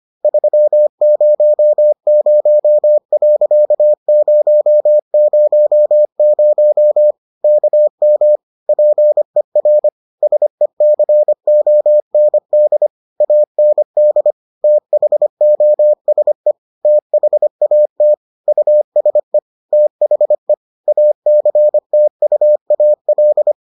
Morse code, heard as a keyed tone, about 25 wpm, 600 Hz. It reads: 300.000 KM PER SECOND AND THOSE THAT USE THE ACTUAL